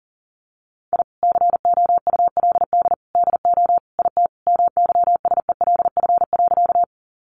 Morse code, heard as a keyed tone, about 37 wpm, 700 Hz. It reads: I COULD DO IT MYSELF.
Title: Sentences from Top 1300 Words 40wpm